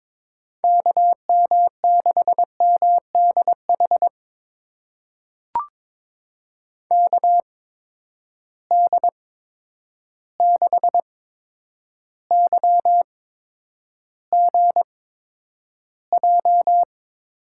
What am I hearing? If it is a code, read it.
Morse code, heard as a keyed tone, about 22 wpm, spaced as for 9 wpm, 700 Hz. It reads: KM6MDH KD6YGJ
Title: USA Call Signs - 6 Characters - 8x Character Spacing - 22wpm